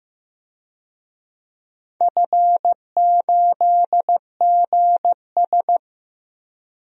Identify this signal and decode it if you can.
Morse code, heard as a keyed tone, about 15 wpm, 700 Hz. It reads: F8GS
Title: DXCC-100 Call Signs - 4 Characters - 15wpm